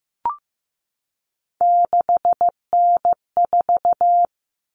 Morse code, heard as a keyed tone, about 15 wpm, 700 Hz. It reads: 6N4
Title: Common Cut Numbers - 3-Digit 15wpm